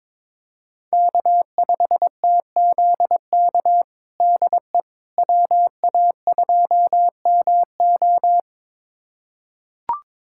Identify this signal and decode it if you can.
Morse code, heard as a keyed tone, about 22 wpm, 700 Hz. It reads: K5TZK DE WA2MO
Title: USA Callsign DE Callsign - 5 Characters - 22wpm